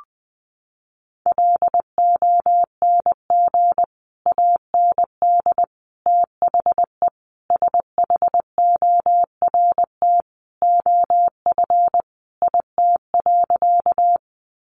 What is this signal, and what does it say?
Morse code, heard as a keyed tone, about 20 wpm, 700 Hz. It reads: LONG AND THE SHORT OF IT.
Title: Sentences from Top 400 Words 20wpm